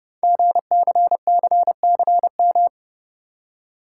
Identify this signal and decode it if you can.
Morse code, heard as a keyed tone, about 30 wpm, 700 Hz. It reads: GCCCM
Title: Code Groups - 5 Letters - 30wpm